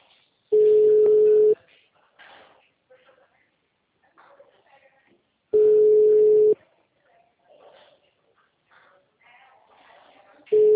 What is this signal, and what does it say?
Ringback tone (Europe, 425 Hz), with speech.
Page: Marilia - SP